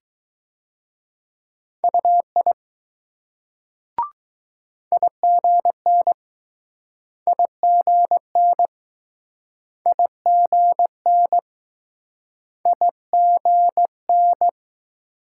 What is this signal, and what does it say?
Morse code, heard as a keyed tone, about 20 wpm, 700 Hz. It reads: UI IGN IGN IGN IGN